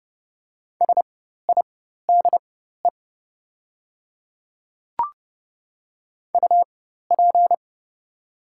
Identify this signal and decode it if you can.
Morse code, heard as a keyed tone, about 30 wpm, spaced as for 14 wpm, 700 Hz. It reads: SIDE UP
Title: 4x Character Spacing - Top 300 Words - 30wpm